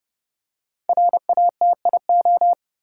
Morse code, heard as a keyed tone, about 30 wpm, 700 Hz. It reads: RATIO
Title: Five Letter Words - 30wpm